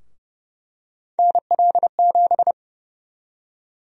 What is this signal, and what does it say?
Morse code, heard as a keyed tone, about 30 wpm, 700 Hz. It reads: NL7
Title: USA Call Sign Prefixes - Repeated 2x - 30wpm